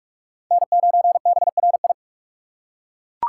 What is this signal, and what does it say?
Morse code, heard as a keyed tone, about 41 wpm, 700 Hz. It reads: N9BRI